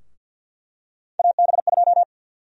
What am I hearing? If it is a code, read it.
Morse code, heard as a keyed tone, about 46 wpm, 700 Hz. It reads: AB2
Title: USA Call Sign Prefixes - 50wpm